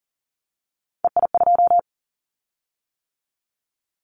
Morse code, heard as a keyed tone, about 38 wpm, 700 Hz. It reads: EI2
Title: Callsign Prefixes for Top 50 DXCC - 40wpm